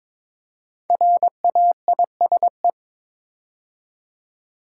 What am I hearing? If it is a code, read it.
Morse code, heard as a keyed tone, about 22 wpm, 700 Hz. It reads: RAISE
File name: 5 Letter Words - First 3 Letters Given - 22wpm